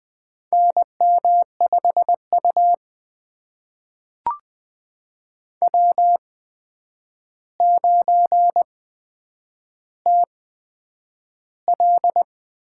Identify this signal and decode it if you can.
Morse code, heard as a keyed tone, about 21 wpm, spaced as for 9 wpm, 700 Hz. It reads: NM5U W9TL
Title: USA Call Signs - 4 Characters - 8x Character Spacing - 20wpm